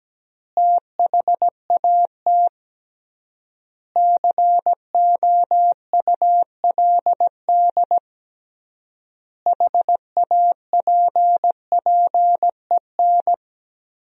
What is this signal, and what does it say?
Morse code, heard as a keyed tone, about 17 wpm, 700 Hz. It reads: THAT COULD HAPPEN
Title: 3x Word-Spacing - Sentences from Top 400 Words - 17wpm